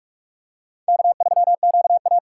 Morse code, heard as a keyed tone, about 46 wpm, 700 Hz. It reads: K3QA
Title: USA Call Signs - 4 Characters - 2x Character Spacing - 45wpm